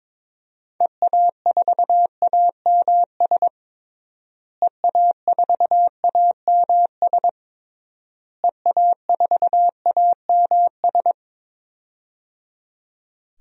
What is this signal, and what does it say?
Morse code, heard as a keyed tone, about 22 wpm, 700 Hz. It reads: EA4AMS EA4AMS EA4AMS
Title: DXCC-100 Call Signs - 6 Characters - Repeated 3x - 22wpm